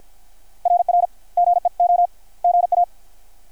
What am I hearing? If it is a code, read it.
Morse code, heard as a keyed tone, about 50 wpm, 700 Hz. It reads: RR GEO GA
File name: archived / CRSnet